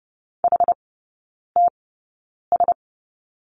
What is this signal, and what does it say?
Morse code, heard as a keyed tone, about 30 wpm, spaced as for 9 wpm, 700 Hz. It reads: HTS
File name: Lesson 18e - Letter m - Three Characters 30wpm